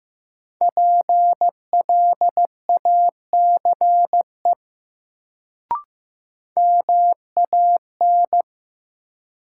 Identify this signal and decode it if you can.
Morse code, heard as a keyed tone, about 15 wpm, 700 Hz. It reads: PLACE MAN
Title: Sentences from Top 200 Words - New Word Drill - 15wpm